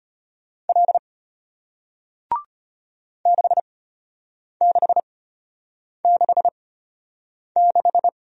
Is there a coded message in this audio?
Yes: Morse code from 0.7 to 8.1 s, heard as a keyed tone, about 33 wpm, spaced as for 8 wpm, 700 Hz. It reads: L 6666